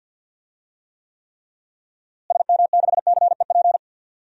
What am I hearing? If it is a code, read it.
Morse code, heard as a keyed tone, about 52 wpm, 700 Hz. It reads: SM6CEP
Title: DXCC-100 Call Signs - 6 Characters - Repeated 3x - 50wpm